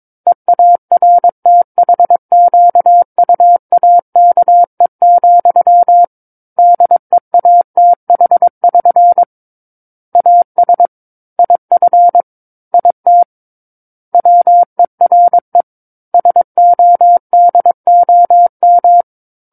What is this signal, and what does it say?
Morse code, heard as a keyed tone, about 22 wpm, 700 Hz. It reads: EARTHQUAKE, DEATH<SN> AS IF IT WERE SODOM
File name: Book One / 13-Chapter 13 - 22 WPM